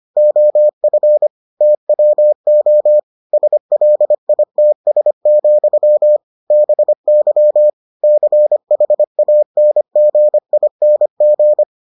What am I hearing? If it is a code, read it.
Morse code, heard as a keyed tone, about 25 wpm, 600 Hz. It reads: OF TWO SLITS, BY CHANGING